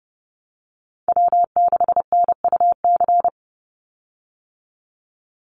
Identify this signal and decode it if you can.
Morse code, heard as a keyed tone, about 30 wpm, 700 Hz. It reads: W6NUC
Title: Call Signs Repeated 3x - 30wpm